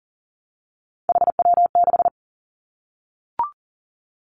Morse code, heard as a keyed tone, about 40 wpm, 700 Hz. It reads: HW6
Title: Three Challenging Letter-Number Combinations 40wpm